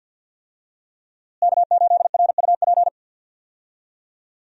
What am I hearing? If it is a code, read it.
Morse code, heard as a keyed tone, about 48 wpm, 700 Hz. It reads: K8RUP
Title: USA Call Signs - 5 Characters - Repeated 3x - 50wpm